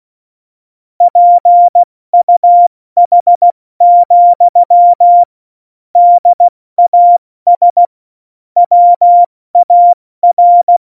Morse code, heard as a keyed tone, about 16 wpm, 700 Hz. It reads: PUH, DAS WAR
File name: Morsekurs / Witze / Witz 29. September 2020, 70 F80